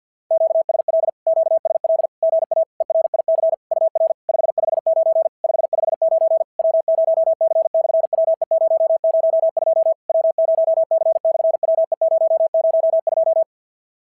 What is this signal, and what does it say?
Morse code, heard as a keyed tone, about 54 wpm, 650 Hz. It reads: QSL QSL GA ERIC UR 559 559 W0C/WE002 W0C/WE002
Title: SOTA QSO Practice 50wpm